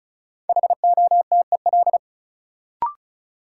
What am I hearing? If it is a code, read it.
Morse code, heard as a keyed tone, about 34 wpm, 700 Hz. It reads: HOTEL
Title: Sentences from Top 1600 Words - New Word Drill - 35wpm